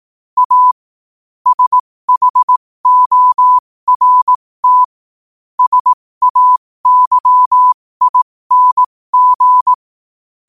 Morse code, heard as a keyed tone, about 18 wpm, spaced as for 15 wpm, 1000 Hz. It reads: A SHORT SAYING
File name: qotdfiles / 15wpm